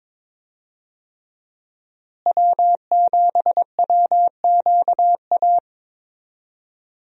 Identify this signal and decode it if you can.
Morse code, heard as a keyed tone, about 22 wpm, 700 Hz. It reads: W7WQA
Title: USA Call Signs - 5 Characters - Repeated 3x - 22wpm